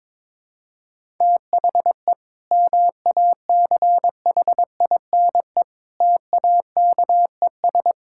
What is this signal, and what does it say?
Morse code, heard as a keyed tone, about 22 wpm, 700 Hz. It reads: THE MACHINE TAKES
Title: Sentences from Top 1000 Words - Review Part 6 - 22wpm